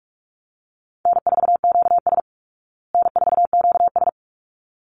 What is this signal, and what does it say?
Morse code, heard as a keyed tone, about 44 wpm, 700 Hz. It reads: N4QS N4QS
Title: Easy Call Signs Repeated 2x - 45wpm